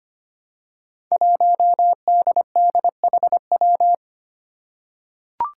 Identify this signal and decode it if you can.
Morse code, heard as a keyed tone, about 25 wpm, 700 Hz. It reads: 1DDHW
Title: Five Challenging Letter-Number Combinations 25wpm